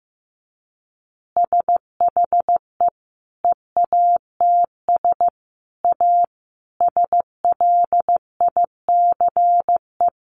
Morse code, heard as a keyed tone, about 15 wpm, 700 Hz. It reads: SHE EATS A SLICE
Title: Additional Easy Phrases 15wpm